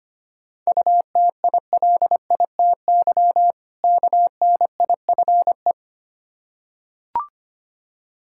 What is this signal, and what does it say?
Morse code, heard as a keyed tone, about 25 wpm, 700 Hz. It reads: UTILITY KNIFE
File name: practice / 25wpm / Difficult Words with Hints - 25wpm